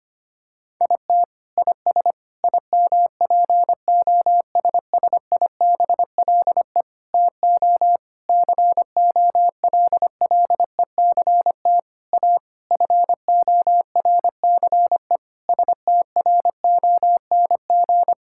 Morse code, heard as a keyed tone, about 25 wpm, 700 Hz. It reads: IT IS IMPOSSIBLE TO COLLECT A FORCE STRONG
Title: Sentences from Top 1600 Words 25wpm